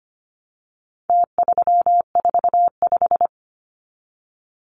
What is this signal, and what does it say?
Morse code, heard as a keyed tone, about 25 wpm, 700 Hz. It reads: T345
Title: Numbers - Contest Style - 4-Digit 25wpm